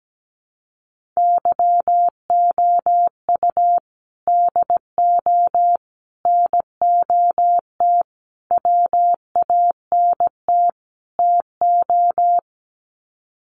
Morse code, heard as a keyed tone, about 17 wpm, 700 Hz. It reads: YOU DO NOT WANT TO
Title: Sets of 5 Words 17wpm